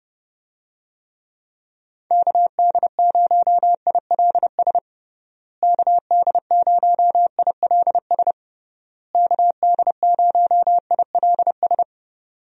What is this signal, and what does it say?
Morse code, heard as a keyed tone, about 29 wpm, 700 Hz. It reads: KD0ILS KD0ILS KD0ILS
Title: USA Call Signs - 6 Characters - Repeated 3x - 30wpm